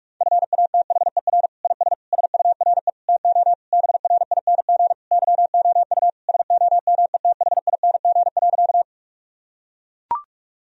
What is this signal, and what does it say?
Morse code, heard as a keyed tone, about 41 wpm, 700 Hz. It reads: FATHER IS SURE TO BRING YOU SOMETHING.
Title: Sentences from Top 1000 Words - Review Part 3 - 45wpm